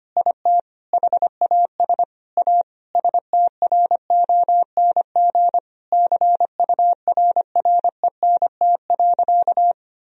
Morse code, heard as a keyed tone, about 25 wpm, 700 Hz. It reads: IT HAS A STRONG CURRENT.
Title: Sentences from Top 1000 Words - Review Part 4 - 25wpm